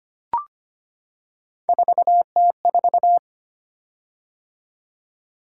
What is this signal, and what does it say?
Morse code, heard as a keyed tone, about 25 wpm, 700 Hz. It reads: 4T4
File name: Three Letter-Number Combinations - 25wpm